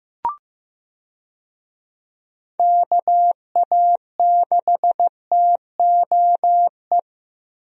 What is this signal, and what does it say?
Morse code, heard as a keyed tone, about 15 wpm, 700 Hz. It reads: KA6TOE
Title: USA Call Signs - 6 Characters - Repeated 2x - 15wpm